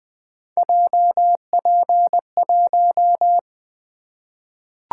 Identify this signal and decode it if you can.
Morse code, heard as a keyed tone, about 20 wpm, 700 Hz. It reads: JP1
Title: Three Letter-Number Combinations - 20wpm